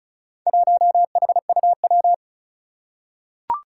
Morse code, heard as a keyed tone, about 36 wpm, 700 Hz. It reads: 1HUW
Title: Four Challenging Letter-Number Combinations 35wpm